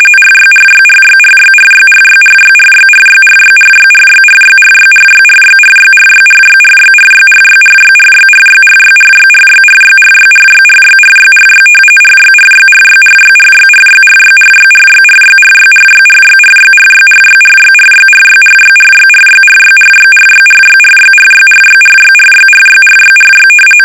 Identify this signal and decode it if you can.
RTTY, about 45.45 baud. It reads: RYRYRYRYRYRYRYRYRYRYRYRYRYRYRYRYRYRYRYRYRYRYRYRYRYRYRYRYRYRYRYRYRYRY RYRYRYRYRYRYRYRYRYRYRYRYRYRYRYRYRYRYRYRYRYRYRYRYRYRYRYRYRYRYRYRYRYRY
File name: rtty / RTTY Test 850